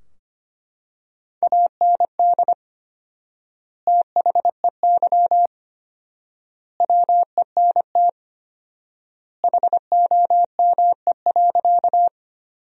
Morse code, heard as a keyed tone, about 25 wpm, 700 Hz. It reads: AND THEY WENT HOME.
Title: 4x Word-Spacing - Sentences from Top 200 Words - 25wpm